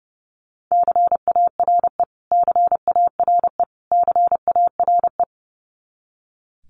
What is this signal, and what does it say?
Morse code, heard as a keyed tone, about 30 wpm, 700 Hz. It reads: CARE CARE CARE
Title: Top 300 Words Repeated 30wpm